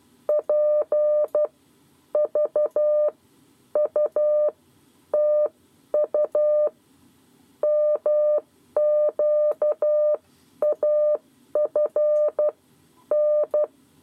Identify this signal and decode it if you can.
Morse code, heard as a keyed tone, about 12 wpm, spaced as for 9 wpm, 600 Hz. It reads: PVUTU MQAFN
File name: 7 Lezione cw 2